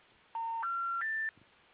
Special information tone.